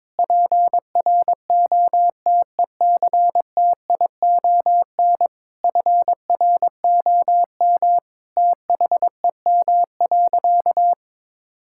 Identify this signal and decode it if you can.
Morse code, heard as a keyed tone, about 21 wpm, 700 Hz. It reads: PROTECTION FROM THEM.
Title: Sentences from Top 1600 Words 22wpm